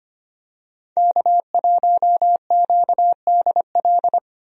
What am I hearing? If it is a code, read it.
Morse code, heard as a keyed tone, about 25 wpm, 700 Hz. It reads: K1QDL
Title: USA Call Signs - 5 Characters - Repeated 3x - 25wpm